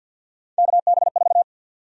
Morse code, heard as a keyed tone, about 47 wpm, 700 Hz. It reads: KB4
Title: USA Call Sign Prefixes - 50wpm